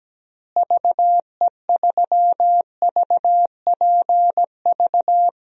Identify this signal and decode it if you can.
Morse code, heard as a keyed tone, about 17 wpm, 700 Hz. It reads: VE3VPV
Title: DXCC-100 Call Signs - 6 Characters - Repeated 3x - 17wpm